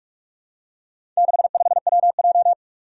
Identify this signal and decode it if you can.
Morse code, heard as a keyed tone, about 44 wpm, 700 Hz. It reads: 65WJ
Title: Four Challenging Letter-Number Combinations 45wpm